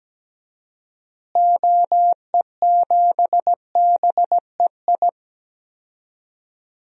Morse code, heard as a keyed tone, about 17 wpm, 700 Hz. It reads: OE7BEI
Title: DXCC-100 Call Signs - 6 Characters - Repeated 2x - 17wpm